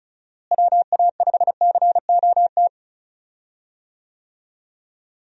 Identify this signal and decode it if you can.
Morse code, heard as a keyed tone, about 34 wpm, 700 Hz. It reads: WA5COT
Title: USA Call Signs - 6 Characters - Repeated 3x - 35wpm